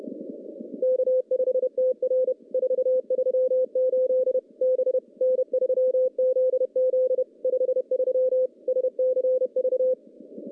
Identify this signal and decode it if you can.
Morse code, heard as a keyed tone, about 30 wpm, 510 Hz. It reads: K5TR 438 B N3ZZ 53 SCV
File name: contests / CW-SS-k5tr- -0400